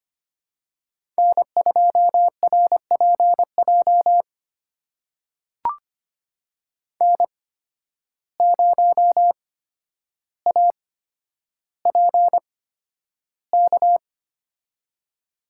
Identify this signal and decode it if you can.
Morse code, heard as a keyed tone, about 25 wpm, spaced as for 11 wpm, 700 Hz. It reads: N2RPJ N0APK
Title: USA Call Signs - 5 Characters - 8x Character Spacing - 25wpm